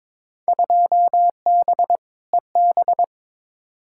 Morse code, heard as a keyed tone, about 22 wpm, 700 Hz. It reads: 2B EB
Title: Field Day Exchanges - 22wpm